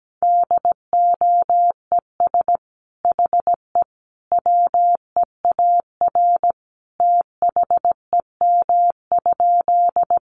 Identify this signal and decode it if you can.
Morse code, heard as a keyed tone, about 17 wpm, 700 Hz. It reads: DOES HE WEAR THEM?
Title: Sentences from Top 1500 Words 17wpm